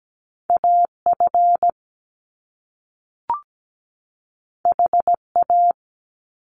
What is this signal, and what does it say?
Morse code, heard as a keyed tone, about 17 wpm, 700 Hz. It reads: AF HA
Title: Top Two Letter Combinations in Words 17wpm